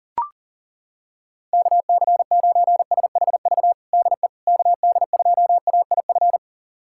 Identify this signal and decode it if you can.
Morse code, heard as a keyed tone, about 40 wpm, 690 Hz. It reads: KC9SHV DE KD2AIF
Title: USA Callsign DE Callsign - 6 Characters - 40wpm